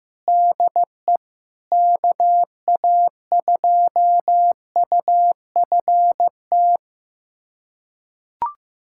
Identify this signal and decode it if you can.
Morse code, heard as a keyed tone, about 15 wpm, 700 Hz. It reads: DE KA2UFT
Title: USA Callsign DE Callsign 15wpm